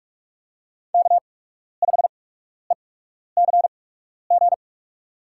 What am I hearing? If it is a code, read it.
Morse code, heard as a keyed tone, about 44 wpm, spaced as for 12 wpm, 690 Hz. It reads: K5ECG
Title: USA Call Signs - 5 Characters - 8x Character Spacing - 45wpm